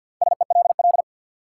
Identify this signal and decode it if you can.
Morse code, heard as a keyed tone, about 52 wpm, 700 Hz. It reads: SELL
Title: Sentences from Top 1100 Words - New Word Drill Alt - 50wpm